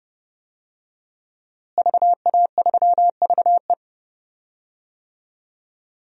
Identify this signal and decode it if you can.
Morse code, heard as a keyed tone, about 32 wpm, 700 Hz. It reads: VA3VE
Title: DXCC-100 Call Signs - 5 Characters - Repeated 3x - 30wpm